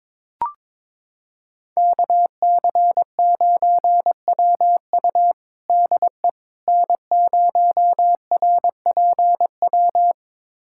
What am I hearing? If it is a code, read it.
Morse code, heard as a keyed tone, about 22 wpm, 700 Hz. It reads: KC9WU DE N0RPW
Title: USA Callsign DE Callsign - 5 Characters - 22wpm